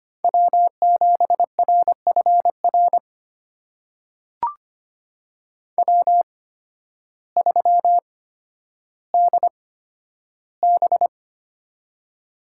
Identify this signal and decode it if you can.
Morse code, heard as a keyed tone, about 25 wpm, spaced as for 12 wpm, 700 Hz. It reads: W7RFR W3DB
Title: USA Call Signs - 8x Character Spacing - 25wpm